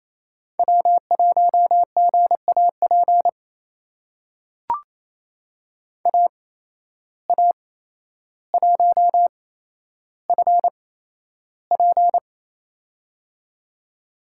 Morse code, heard as a keyed tone, about 28 wpm, spaced as for 12 wpm, 700 Hz. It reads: W1GAP AA1FP
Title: USA Call Signs - 5 Characters - 8x Character Spacing - 28wpm